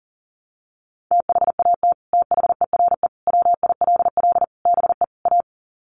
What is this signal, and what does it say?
Morse code, heard as a keyed tone, about 40 wpm, 700 Hz. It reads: THAT THERE WILL BE A
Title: Sets of 5 Words Encore2 40wpm